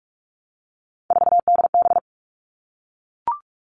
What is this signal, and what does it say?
Morse code, heard as a keyed tone, about 44 wpm, 700 Hz. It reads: 4DB
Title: All Cut Numbers - 3-Digit 45wpm